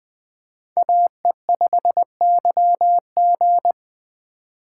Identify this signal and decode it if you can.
Morse code, heard as a keyed tone, about 20 wpm, 700 Hz. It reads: AE5YG
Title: Speed Racing - USA 2x2 Call Signs 17wpm